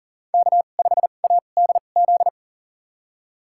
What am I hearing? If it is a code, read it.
Morse code, heard as a keyed tone, about 39 wpm, spaced as for 29 wpm, 700 Hz. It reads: K5ADZ
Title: USA Call Signs - 5 Characters - 2x Character Spacing - 40wpm